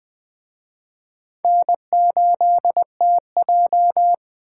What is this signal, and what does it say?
Morse code, heard as a keyed tone, about 24 wpm, 700 Hz. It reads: N8TJ
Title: Speed Racing - USA 1x2 Call Signs 20wpm